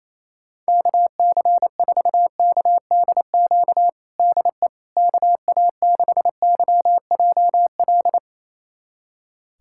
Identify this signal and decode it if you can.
Morse code, heard as a keyed tone, about 28 wpm, 700 Hz. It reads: KC4KDQ DE KA6YJL
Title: USA Callsign DE Callsign - 6 Characters - 28wpm